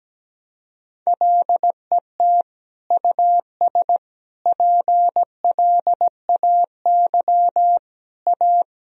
Morse code, heard as a keyed tone, about 17 wpm, 700 Hz. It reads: LET US PLAY A
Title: Sentences from Top 1000 Words - Review Part 3 - 17wpm